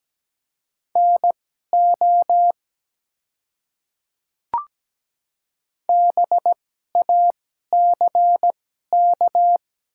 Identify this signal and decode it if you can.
Morse code, heard as a keyed tone, about 17 wpm, spaced as for 12 wpm, 700 Hz. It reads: NO BACK
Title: 2x Character Spacing - Top 200 Words - 17wpm